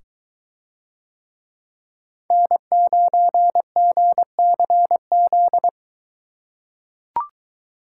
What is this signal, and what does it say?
Morse code, heard as a keyed tone, about 21 wpm, 700 Hz. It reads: N9GCZ